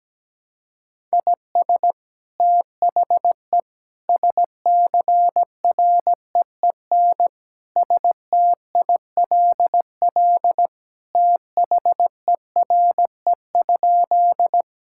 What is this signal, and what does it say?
Morse code, heard as a keyed tone, about 17 wpm, 700 Hz. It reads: IS THE SCREEN STILL THERE?
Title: Sentences from Top 1000 Words - Review Part 6 - 17wpm